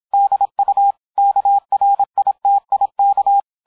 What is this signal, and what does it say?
Morse code, heard as a keyed tone, about 27 wpm, 800 Hz, with radio noise. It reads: DU KRITIK